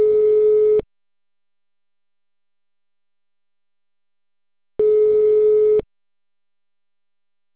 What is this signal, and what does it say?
Ringback tone (Europe, 425 Hz).